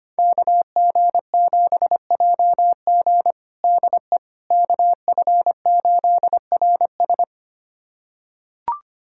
Morse code, heard as a keyed tone, about 25 wpm, 700 Hz. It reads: KG7JG DE KF8RS
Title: USA Callsign DE Callsign - 5 Characters - 25wpm